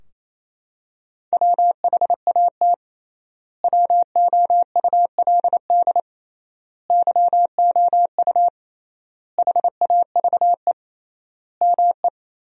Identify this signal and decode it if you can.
Morse code, heard as a keyed tone, about 27 wpm, 700 Hz. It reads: WHAT WOULD YOU HAVE ME
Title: 3x Word-Spacing - Sentences from Top 200 Words - 28wpm